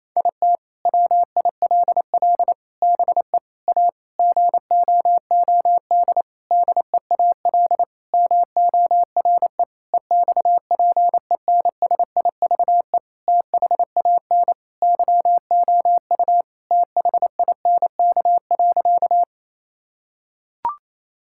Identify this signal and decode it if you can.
Morse code, heard as a keyed tone, about 28 wpm, 700 Hz. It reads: IT WILL BE A GOOD DEAL MORE EXPENSIVE THAN YOU THINK.
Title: Sentences from Top 1000 Words - Review Part 6 - 28wpm